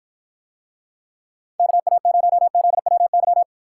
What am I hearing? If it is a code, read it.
Morse code, heard as a keyed tone, about 53 wpm, 700 Hz. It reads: KA0ZWY